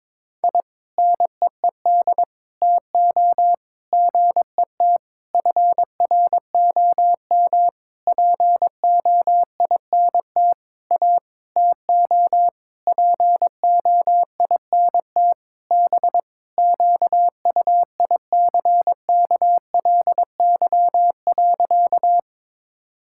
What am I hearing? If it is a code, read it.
Morse code, heard as a keyed tone, about 22 wpm, 700 Hz. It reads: I NEED TO GET FROM POINT A TO POINT B QUICKLY.
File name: Sentences with 7-Word Phrases - 22wpm